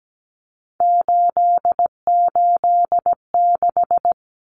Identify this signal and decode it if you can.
Morse code, heard as a keyed tone, about 17 wpm, 700 Hz. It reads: 886
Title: Numbers - 3-Digit 17wpm